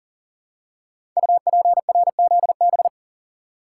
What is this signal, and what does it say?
Morse code, heard as a keyed tone, about 42 wpm, 700 Hz. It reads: UPRZB
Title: Code Groups - 5 Letters - 40wpm